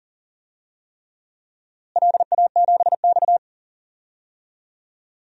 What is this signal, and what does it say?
Morse code, heard as a keyed tone, about 40 wpm, 700 Hz. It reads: LA7X
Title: DXCC-100 Call Signs - 4 Characters - Repeated 3x - 40wpm